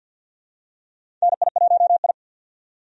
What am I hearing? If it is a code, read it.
Morse code, heard as a keyed tone, about 52 wpm, 700 Hz. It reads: NI1I
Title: USA Call Signs - 4 Characters - 50wpm